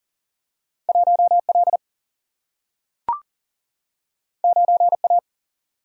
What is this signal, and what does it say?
Morse code, heard as a keyed tone, about 40 wpm, 700 Hz. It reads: 1 L 9 A
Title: Single-Letter Single-Number 40wpm